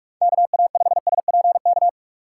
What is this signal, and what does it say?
Morse code, heard as a keyed tone, about 59 wpm, spaced as for 52 wpm, 700 Hz. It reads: KA5SPK
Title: Speed Racing - USA 2x3 Call Signs 45wpm